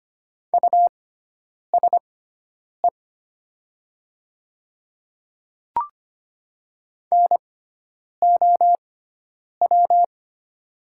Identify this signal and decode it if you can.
Morse code, heard as a keyed tone, about 25 wpm, spaced as for 9 wpm, 700 Hz. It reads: USE NOW
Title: 6x Character Spacing - Top 100 Words - 25wpm